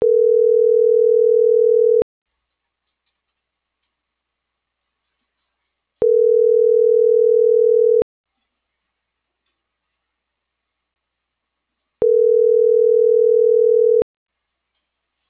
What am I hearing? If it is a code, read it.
Ringback tone (North America), 440 + 480 Hz.